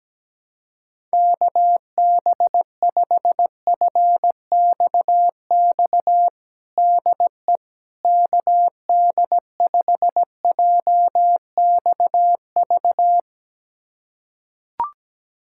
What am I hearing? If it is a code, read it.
Morse code, heard as a keyed tone, about 17 wpm, 700 Hz. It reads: KB5FXX DE KD5JXV
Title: USA Callsign DE Callsign - 6 Characters - 17wpm